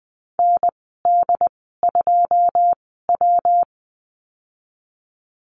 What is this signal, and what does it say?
Morse code, heard as a keyed tone, about 20 wpm, spaced as for 15 wpm, 700 Hz. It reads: ND2W
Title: 179 Lesson 29g - Slash - Call Signs 20wpm